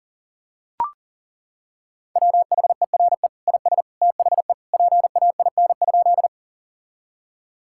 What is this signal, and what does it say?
Morse code, heard as a keyed tone, about 37 wpm, 700 Hz. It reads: WHERE IS THE PAIN?
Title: Sentences from Top 1600 Words 40wpm